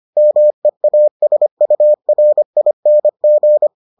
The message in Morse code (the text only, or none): MEASURING